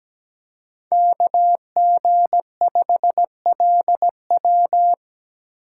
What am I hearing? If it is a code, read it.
Morse code, heard as a keyed tone, about 17 wpm, 700 Hz. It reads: KG5LW
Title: USA Call Signs - 5 Characters - 3x Character Spacing - 17wpm